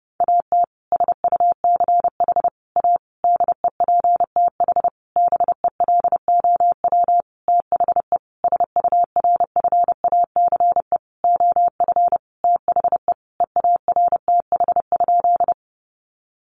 Morse code, heard as a keyed tone, about 30 wpm, 700 Hz. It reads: AT SUCH A DEPTH BELOW THE SURFACE OF THE EARTH?
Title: Sentences from Top 1200 Words 30wpm